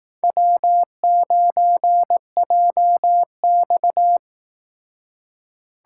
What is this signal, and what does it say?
Morse code, heard as a keyed tone, about 18 wpm, 700 Hz. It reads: W9JX
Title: Speed Racing - USA 1x2 Call Signs 15wpm